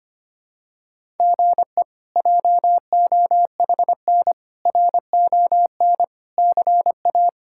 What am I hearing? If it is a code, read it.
Morse code, heard as a keyed tone, about 25 wpm, 700 Hz. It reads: GE JOHN RON CA
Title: K1USN SST QSO Practice - 25wpm